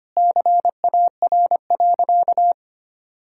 Morse code, heard as a keyed tone, about 25 wpm, 700 Hz. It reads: CAR.
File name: practice / 25wpm / Sentences with 4-Word Phrases - 25wpm